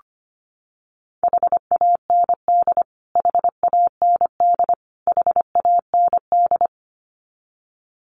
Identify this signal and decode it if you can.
Morse code, heard as a keyed tone, about 25 wpm, 700 Hz. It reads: HAND HAND HAND
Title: Top 200 Words Repeated 25wpm